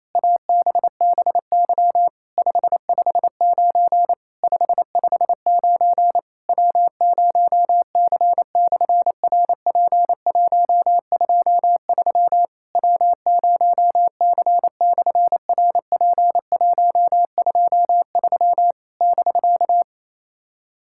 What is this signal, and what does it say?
Morse code, heard as a keyed tone, about 28 wpm, 700 Hz. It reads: ABBY 559 559 W0C/RP123 W0C/RP123 <BK>
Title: SOTA QSO Practice 28wpm